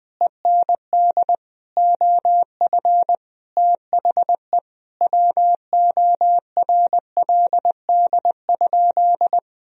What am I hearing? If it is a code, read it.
Morse code, heard as a keyed tone, about 21 wpm, 700 Hz. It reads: END OF THE WORLD?
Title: Sentences from Top 200 Words and Abbreviations 20wpm